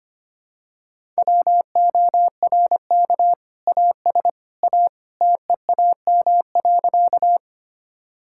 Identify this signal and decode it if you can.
Morse code, heard as a keyed tone, about 25 wpm, 700 Hz. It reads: WORK AS A TEAM.
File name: practice / 25wpm / Sentences with 2-Word Phrases - 25wpm